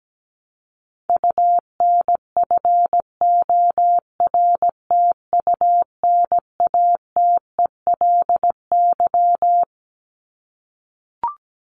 Morse code, heard as a keyed tone, about 17 wpm, 700 Hz. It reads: UNFORTUNATELY
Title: LY Suffix - Top 100 Words with Base Words 17wpm